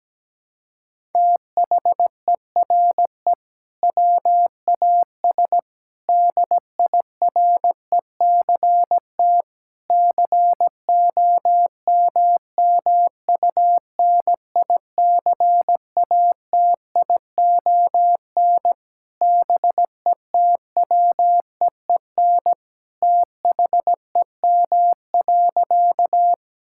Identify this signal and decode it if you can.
Morse code, heard as a keyed tone, about 17 wpm, 700 Hz. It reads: THERE WAS DIRECT COMMUNICATION BETWEEN THEM.
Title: Sentences from Top 1000 Words - Review Part 6 - 17wpm